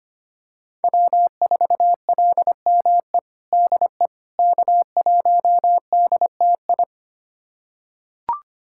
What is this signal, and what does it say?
Morse code, heard as a keyed tone, about 25 wpm, 700 Hz. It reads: W4LME DE K1DTI
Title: USA Callsign DE Callsign - 5 Characters - 25wpm